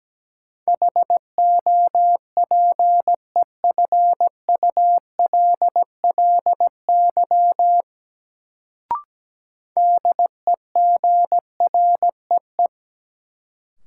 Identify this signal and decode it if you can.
Morse code, heard as a keyed tone, about 17 wpm, 700 Hz. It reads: HOPEFULLY DEGREE
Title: Sentences from Top 1500 Words - New Word Drill - 17wpm